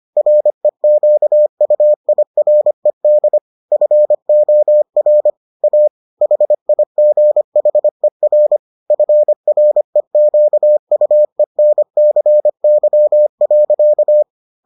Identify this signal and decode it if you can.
Morse code, heard as a keyed tone, about 25 wpm, 600 Hz. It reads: REQUIRED FOR A HIGHER FREQUENCY.